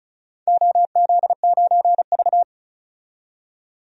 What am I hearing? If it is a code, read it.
Morse code, heard as a keyed tone, about 36 wpm, 700 Hz. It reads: OZ9V
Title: DXCC-100 Call Signs - 4 Characters - 35wpm